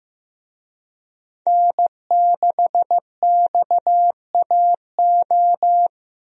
Morse code, heard as a keyed tone, about 15 wpm, 700 Hz. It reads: N6XAO
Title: USA Call Signs - 5 Characters - 15wpm